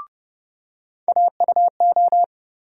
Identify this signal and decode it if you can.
Morse code, heard as a keyed tone, about 29 wpm, 700 Hz. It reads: AUO
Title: USA Call Sign Postfixes - 3 Letters 30wpm